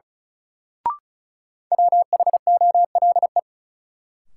Morse code, heard as a keyed tone, about 36 wpm, 700 Hz. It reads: WHOLE
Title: Sets of 2 Words - New Word Drill - 35wpm